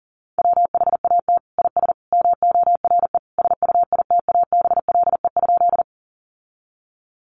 Morse code, heard as a keyed tone, about 40 wpm, 700 Hz. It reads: WHAT IS MORE SUITABLE?
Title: Sentences from Top 1400 Words 40wpm